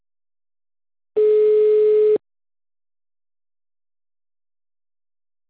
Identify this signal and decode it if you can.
Ringback tone (Europe, 425 Hz).
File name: out- -2006- - - .98